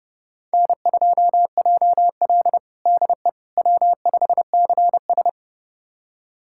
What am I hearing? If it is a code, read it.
Morse code, heard as a keyed tone, about 31 wpm, 700 Hz. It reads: N2JL DE W5CS
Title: USA Callsign DE Callsign - 4 Characters - 30wpm